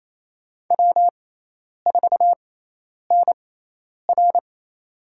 Morse code, heard as a keyed tone, about 28 wpm, spaced as for 10 wpm, 700 Hz. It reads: W4NR
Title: USA Call Signs - 4 Characters - 6x Character Spacing - 28wpm